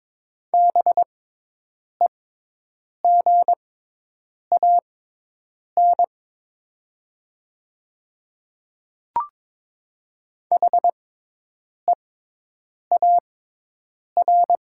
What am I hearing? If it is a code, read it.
Morse code, heard as a keyed tone, about 22 wpm, spaced as for 8 wpm, 700 Hz. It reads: BEGAN HEAR
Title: 6x Character Spacing - Top 300 Words - 22wpm